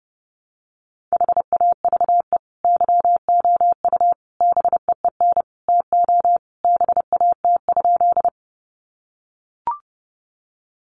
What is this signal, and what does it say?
Morse code, heard as a keyed tone, about 30 wpm, 700 Hz. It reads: HAVE YOU BEEN TO BAT?
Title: Sentences from Top 1500 Words 30wpm